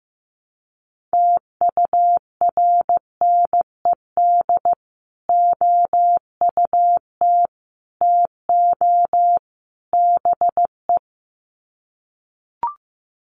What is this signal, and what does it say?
Morse code, heard as a keyed tone, about 15 wpm, 700 Hz. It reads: TURNED OUT TO BE
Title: Sets of 4 Words 15wpm